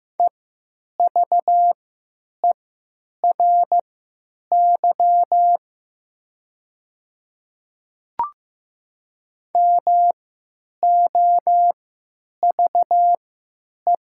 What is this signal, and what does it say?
Morse code, heard as a keyed tone, about 15 wpm, spaced as for 9 wpm, 700 Hz. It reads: EVERY MOVE
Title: 3x Character Spacing - Top 500 Words - 15wpm